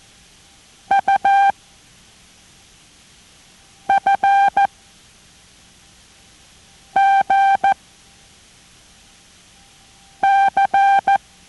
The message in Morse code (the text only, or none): UFGC